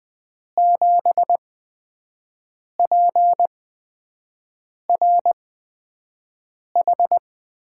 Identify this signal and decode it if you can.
Morse code, heard as a keyed tone, about 20 wpm, spaced as for 5 wpm, 700 Hz. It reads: 7PRH
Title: USA Call Signs - 6 Characters - 8x Character Spacing - 20wpm